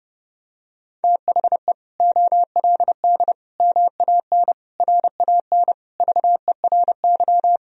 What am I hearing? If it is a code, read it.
Morse code, heard as a keyed tone, about 31 wpm, 700 Hz. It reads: THE OLD MAN RAN VERY
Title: Sentences from Top 1000 Words - Review Part 3 - 30wpm